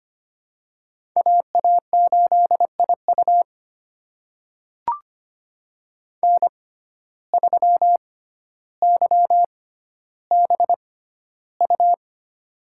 Morse code, heard as a keyed tone, about 25 wpm, spaced as for 13 wpm, 700 Hz. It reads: AA8IU N3YBU
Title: USA Call Signs - 5 Characters - 6x Character Spacing - 25wpm